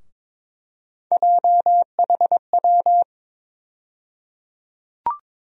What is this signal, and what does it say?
Morse code, heard as a keyed tone, about 22 wpm, 700 Hz. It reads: JHW
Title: USA Call Sign Postfixes - 3 Letters 22wpm